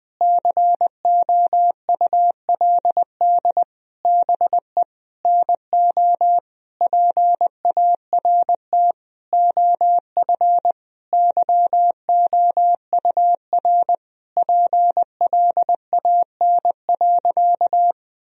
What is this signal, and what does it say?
Morse code, heard as a keyed tone, about 21 wpm, 700 Hz. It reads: COULD BE NO PART OF YOUR PLAN.
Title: Sentences from Top 1000 Words - Review Part 2 - 20wpm